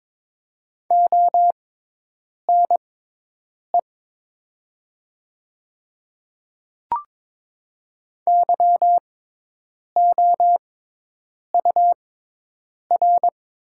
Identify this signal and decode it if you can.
Morse code, heard as a keyed tone, about 22 wpm, spaced as for 8 wpm, 700 Hz. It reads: ONE YOUR
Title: 6x Character Spacing - Top 100 Words - 22wpm